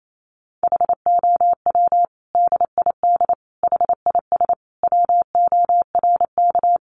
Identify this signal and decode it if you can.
Morse code, heard as a keyed tone, about 28 wpm, 700 Hz. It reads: HOW DID HIS WORK
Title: Sentences from Top 900 Words 28wpm